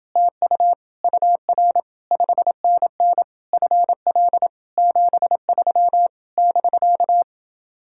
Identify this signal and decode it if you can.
Morse code, heard as a keyed tone, about 30 wpm, 700 Hz. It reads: TU UR 5NN FL 73 <BK>